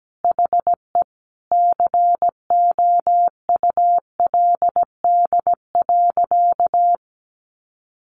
Morse code, heard as a keyed tone, about 17 wpm, 700 Hz. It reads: HE COULD.
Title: Sentences from Top 500 Words 17wpm